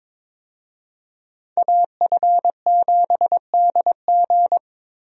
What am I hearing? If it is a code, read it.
Morse code, heard as a keyed tone, about 21 wpm, 700 Hz. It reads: AF7DG
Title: USA Call Signs - 5 Characters - Repeated 2x - 22wpm